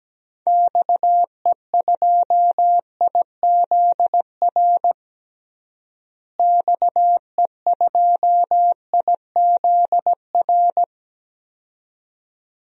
Morse code, heard as a keyed tone, about 17 wpm, 700 Hz. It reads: XE2IZR XE2IZR
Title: DXCC-100 Call Signs - 6 Characters - Repeated 2x - 17wpm